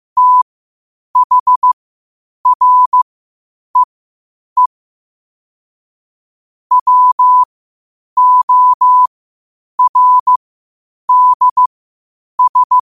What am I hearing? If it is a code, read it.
Morse code, heard as a keyed tone, about 15 wpm, spaced as for 8 wpm, 1000 Hz. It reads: THREE WORDS
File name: qotdfiles / 8wpm